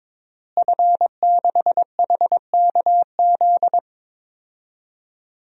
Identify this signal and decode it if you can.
Morse code, heard as a keyed tone, about 22 wpm, 700 Hz. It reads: F6HKZ
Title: DXCC-100 Call Signs - 5 Characters - Repeated 2x - 22wpm